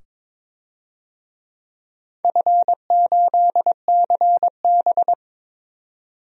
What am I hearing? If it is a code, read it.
Morse code, heard as a keyed tone, about 22 wpm, 700 Hz. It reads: F8CB
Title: DXCC-100 Call Signs - 4 Characters - 22wpm